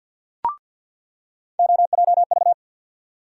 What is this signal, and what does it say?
Morse code, heard as a keyed tone, about 46 wpm, 700 Hz. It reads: OJV
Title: USA Call Sign Postfixes - 3 Letters 50wpm